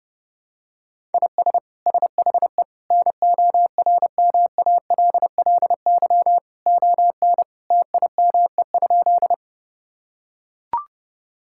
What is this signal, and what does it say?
Morse code, heard as a keyed tone, about 32 wpm, 700 Hz. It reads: IS SHE NORMALLY ON TIME?
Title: Sentences from Top 1000 Words - Review Part 6 - 30wpm